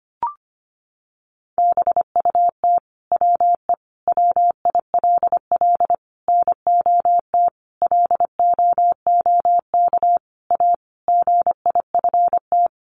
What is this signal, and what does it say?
Morse code, heard as a keyed tone, about 25 wpm, 700 Hz. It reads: BUT WE WILL NOT LOOK A GIFT
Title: Sentences from Top 1500 Words 25wpm